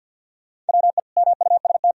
Morse code, heard as a keyed tone, about 46 wpm, 700 Hz. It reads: WE MUST
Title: Sentences from Top 300 Words and Abbreviations 50wpm